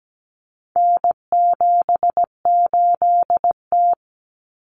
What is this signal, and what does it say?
Morse code, heard as a keyed tone, about 17 wpm, 700 Hz. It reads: N78T
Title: Common Cut Numbers - 4-Digit 17wpm